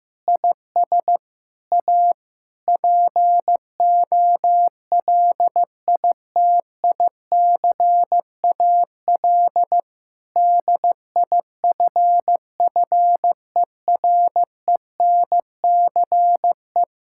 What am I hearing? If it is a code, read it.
Morse code, heard as a keyed tone, about 15 wpm, 700 Hz. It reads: IS A POLITICAL DIFFERENCE
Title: Sentences from Top 1000 Words - Review Part 5 - 15wpm